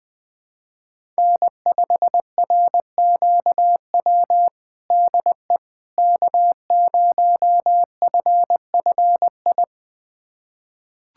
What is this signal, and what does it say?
Morse code, heard as a keyed tone, about 21 wpm, 700 Hz. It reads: N5RQW DE K0FFI
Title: USA Callsign DE Callsign 20wpm